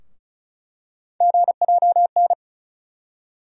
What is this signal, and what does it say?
Morse code, heard as a keyed tone, about 34 wpm, 700 Hz. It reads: GJN
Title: USA Call Sign Postfixes - 3 Letters 35wpm